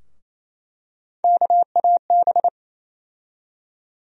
Morse code, heard as a keyed tone, about 28 wpm, 700 Hz. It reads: KAB
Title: USA Call Sign Postfixes - 3 Letters 28wpm